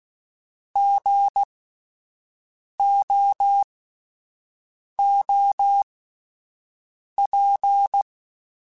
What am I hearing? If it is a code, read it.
Morse code, heard as a keyed tone, about 16 wpm, spaced as for 5 wpm, 800 Hz. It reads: GOOP
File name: Spaced Alphas / Spaced Alphas mp3s / Gs sp @10